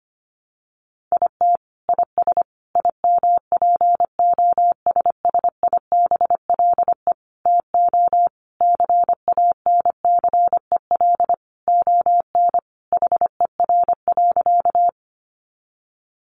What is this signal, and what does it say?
Morse code, heard as a keyed tone, about 25 wpm, 700 Hz. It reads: IT IS IMPOSSIBLE TO CANCEL ON HER.
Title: Sentences from Top 1200 Words 25wpm